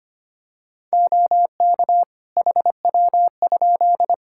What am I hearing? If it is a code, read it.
Morse code, heard as a keyed tone, about 25 wpm, 700 Hz. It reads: OK HW?
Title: K1USN SST QSO Practice 25wpm